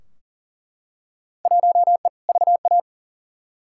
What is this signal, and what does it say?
Morse code, heard as a keyed tone, about 37 wpm, 700 Hz. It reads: 1E VA
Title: Field Day Exchanges - 40wpm